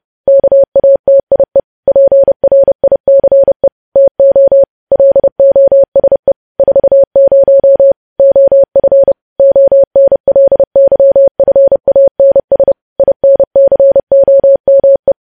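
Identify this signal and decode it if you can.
Morse code, heard as a keyed tone, about 30 wpm, 550 Hz. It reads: KATIE PRICE TO LOSE 40 OF ONLYFANS INCOME
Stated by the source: Morse news headlines at 30 wpm.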